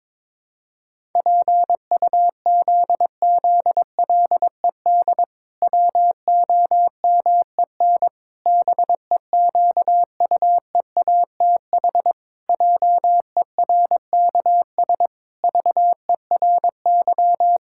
Morse code, heard as a keyed tone, about 22 wpm, 700 Hz. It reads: PUZZLED WOMEN BEQUEATH JERKS VERY
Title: Pangram Challenge v2 - 22wpm